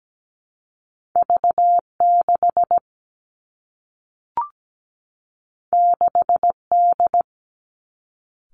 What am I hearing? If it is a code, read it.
Morse code, heard as a keyed tone, about 17 wpm, 700 Hz. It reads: V6 6D
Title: Two Challenging Letter-Number Combinations 17wpm